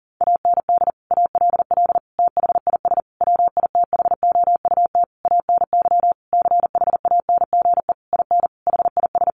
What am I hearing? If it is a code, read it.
Morse code, heard as a keyed tone, about 40 wpm, 700 Hz. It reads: AND ALL THIS WITHOUT ANY CHANGE IN HIS
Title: Sentences from Top 1200 Words 40wpm